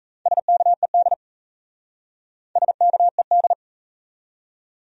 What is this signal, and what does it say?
Morse code, heard as a keyed tone, about 40 wpm, 700 Hz. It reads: SKED SKED